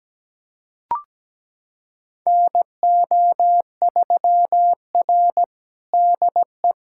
Morse code, heard as a keyed tone, about 17 wpm, 700 Hz. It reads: NO3R DE
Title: USA Callsign DE Callsign - 4 Characters - 17wpm